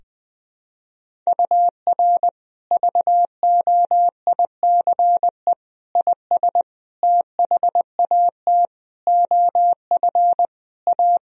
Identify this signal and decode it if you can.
Morse code, heard as a keyed tone, about 20 wpm, 700 Hz. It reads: UR VOICE IS THAT OF A
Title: Sentences from Top 500 Words and Abbreviations 20wpm